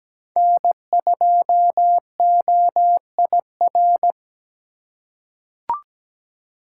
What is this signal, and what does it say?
Morse code, heard as a keyed tone, about 17 wpm, 700 Hz. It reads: N2OIR
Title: USA Call Signs - 5 Characters - 3x Character Spacing - 17wpm